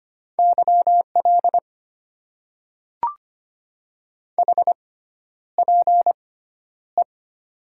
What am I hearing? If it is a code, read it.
Morse code, heard as a keyed tone, about 25 wpm, spaced as for 11 wpm, 700 Hz. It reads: YL HPE
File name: Top 90 QSO Elements - 6x Character Spacing - 25wpm